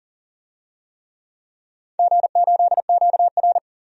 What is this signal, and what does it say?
Morse code, heard as a keyed tone, about 40 wpm, 700 Hz. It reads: G8QR
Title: DXCC-100 Call Signs - 4 Characters - 40wpm